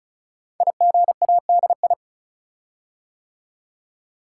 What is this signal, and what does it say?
Morse code, heard as a keyed tone, about 36 wpm, 700 Hz. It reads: IGADI
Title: Code Groups - 5 Letters - 35wpm